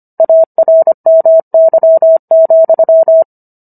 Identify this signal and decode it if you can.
Morse code, heard as a keyed tone, about 25 wpm, 650 Hz. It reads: ARMY,